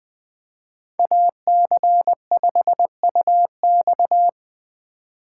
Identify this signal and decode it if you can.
Morse code, heard as a keyed tone, about 24 wpm, 700 Hz. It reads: AC5UX
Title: Speed Racing - USA 2x2 Call Signs 20wpm